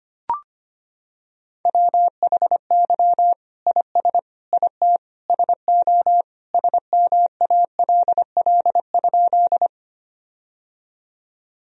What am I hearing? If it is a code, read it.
Morse code, heard as a keyed tone, about 25 wpm, 700 Hz. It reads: WHY IS IT SO SMALL?
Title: Sentences from Top 1000 Words - Review Part 1 - 25wpm